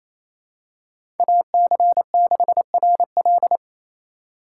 Morse code, heard as a keyed tone, about 28 wpm, 700 Hz. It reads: AC6RL
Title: USA Call Signs - 5 Characters - 2x Character Spacing - 28wpm